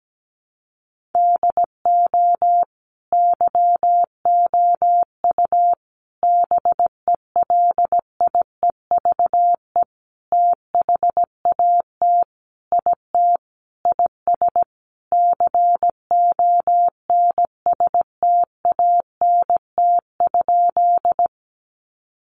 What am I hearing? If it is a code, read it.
Morse code, heard as a keyed tone, about 17 wpm, 700 Hz. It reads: DO YOU BELIEVE THAT IT IS CONSTANT?
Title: Sentences from Top 1500 Words 17wpm